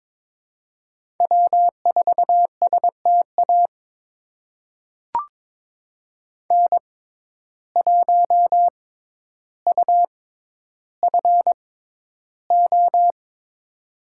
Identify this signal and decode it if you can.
Morse code, heard as a keyed tone, about 22 wpm, spaced as for 11 wpm, 700 Hz. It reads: W4STA N1UFO
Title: USA Call Signs - 5 Characters - 6x Character Spacing - 22wpm